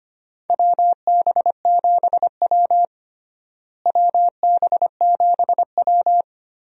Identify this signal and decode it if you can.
Morse code, heard as a keyed tone, about 25 wpm, 700 Hz. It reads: WB7W WB7W
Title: USA Call Signs - 4 Characters - Repeated 3x - 25wpm